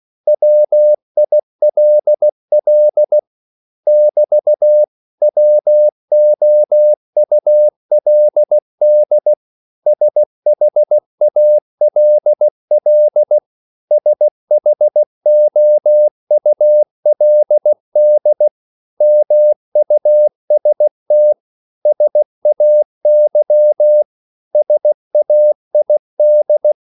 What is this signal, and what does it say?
Morse code, heard as a keyed tone, about 16 wpm, 600 Hz. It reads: WILL <BT> WOULD SHALL SHOULD MUST SAY SAID
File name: CW / Practice / 500MostCommonEnglishWords16